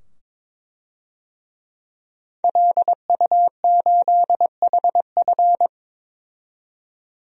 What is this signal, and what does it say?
Morse code, heard as a keyed tone, about 22 wpm, 700 Hz. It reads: LU8HF
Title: DXCC-100 Call Signs - 5 Characters - 22wpm